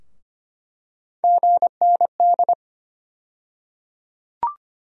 Morse code, heard as a keyed tone, about 25 wpm, 700 Hz. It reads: GND
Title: USA Call Sign Postfixes - 3 Letters 25wpm